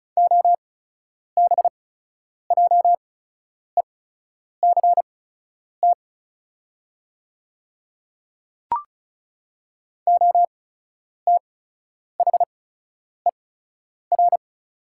Morse code, heard as a keyed tone, about 36 wpm, spaced as for 10 wpm, 690 Hz. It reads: OBJECT OTHER
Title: 8x Character Spacing - Top 500 Words - 35wpm